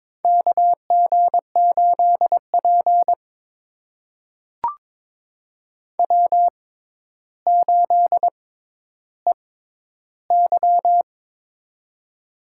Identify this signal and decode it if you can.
Morse code, heard as a keyed tone, about 23 wpm, spaced as for 11 wpm, 700 Hz. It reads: KG8P W8EY
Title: USA Call Signs - 4 Characters - 6x Character Spacing - 22wpm